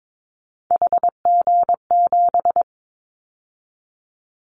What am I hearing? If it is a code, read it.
Morse code, heard as a keyed tone, about 22 wpm, 700 Hz. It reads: HG7
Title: Callsign Prefixes for Top 50 DXCC - 22wpm